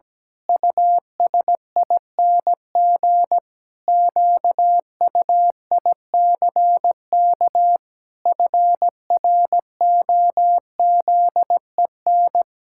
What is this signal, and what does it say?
Morse code, heard as a keyed tone, about 17 wpm, 700 Hz. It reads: USING QUICK FROZEN
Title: Pangram Challenge v2 - 17wpm